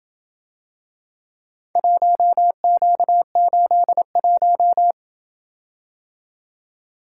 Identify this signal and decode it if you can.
Morse code, heard as a keyed tone, about 26 wpm, 700 Hz. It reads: 1Q81